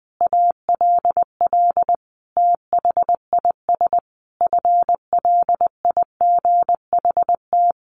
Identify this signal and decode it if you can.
Morse code, heard as a keyed tone, about 20 wpm, 700 Hz. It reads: ALL THIS FLIGHT
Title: Sentences from Top 1300 Words 20wpm